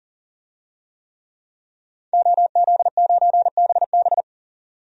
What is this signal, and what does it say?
Morse code, heard as a keyed tone, about 37 wpm, 700 Hz. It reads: OZ9BB
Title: DXCC-100 Call Signs - 5 Characters - 40wpm